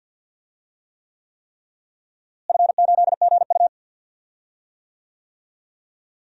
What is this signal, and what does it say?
Morse code, heard as a keyed tone, about 50 wpm, 700 Hz. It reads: F8GU
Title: DXCC-100 Call Signs - 4 Characters - Repeated 3x - 50wpm